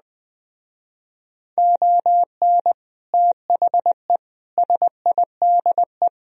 Morse code, heard as a keyed tone, about 20 wpm, 700 Hz. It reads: ON THE SIDE
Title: Speed-Racing - Sets of 5 Words 17wpm